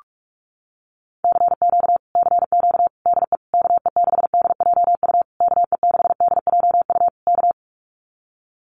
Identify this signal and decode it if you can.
Morse code, heard as a keyed tone, about 44 wpm, 700 Hz. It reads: CQ CQ DE KE6DJU KE6DJU K
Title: Common QSO Phrases Encore 45wpm